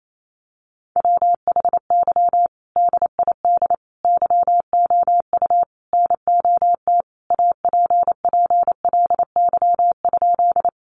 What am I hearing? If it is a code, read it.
Morse code, heard as a keyed tone, about 28 wpm, 700 Hz. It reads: WHY DID YOU NOT APPLY?
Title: Sentences from Top 1200 Words 28wpm